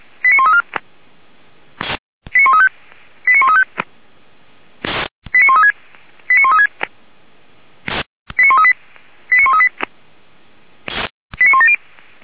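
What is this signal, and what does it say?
Five-tone selective call (ZVEI), heard as quick runs of five tones, tones of about 80 ms. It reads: 89134 89135 89135 89136 89136 89138 89138 89170
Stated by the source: SDR recording from feuerwehr-orlamuende.org, fire brigade Orlamünde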